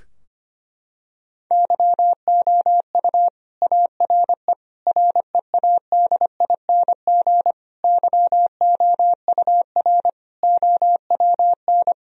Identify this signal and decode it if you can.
Morse code, heard as a keyed tone, about 25 wpm, 700 Hz. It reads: YOU ARE READING YOUR OWN
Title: Sentences from Top 1000 Words - Review Part 3 - 25wpm